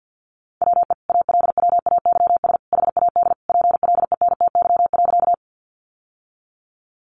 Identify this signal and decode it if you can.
Morse code, heard as a keyed tone, about 50 wpm, 700 Hz. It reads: WE ALWAYS HAD PLENTY.
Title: Sentences from Top 1300 Words 50wpm